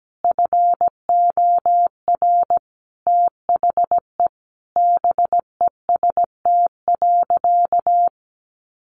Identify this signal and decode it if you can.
Morse code, heard as a keyed tone, about 17 wpm, 700 Hz. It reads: FOR THE BEST.
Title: Sentences from Top 1000 Words 17wpm